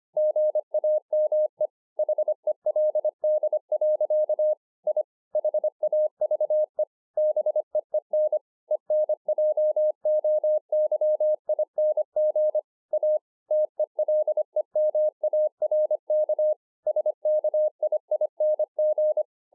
Morse code, heard as a keyed tone, about 24 wpm, 610 Hz. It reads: GAME HELD. I HAVE BEEN ENJOYING A TELEMARK SKIING